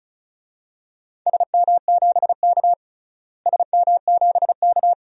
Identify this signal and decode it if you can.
Morse code, heard as a keyed tone, about 34 wpm, 700 Hz. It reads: SM7K SM7K
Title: DXCC-100 Call Signs - 4 Characters - Repeated 3x - 35wpm